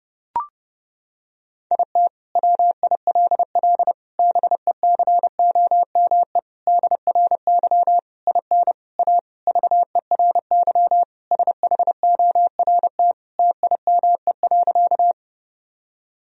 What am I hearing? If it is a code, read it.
Morse code, heard as a keyed tone, about 30 wpm, 700 Hz. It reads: IT WILL BECOME DRY IN A VERY SHORT TIME.
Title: Sentences from Top 1000 Words - Review Part 3 - 30wpm